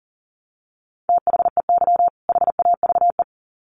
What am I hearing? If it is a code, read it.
Morse code, heard as a keyed tone, about 40 wpm, 700 Hz. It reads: THEY HAVE
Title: Sets of 2 Words 40wpm